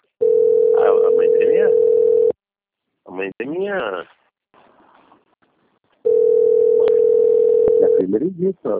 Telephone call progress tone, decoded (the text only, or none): ringback tone